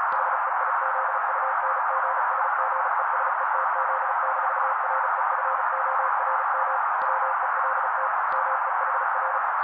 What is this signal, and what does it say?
Morse code, heard as a keyed tone, about 36 wpm, 530 Hz. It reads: HEIZUNGSGESETZ-REFORM WELT MEHR
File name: daily / 35wpm noise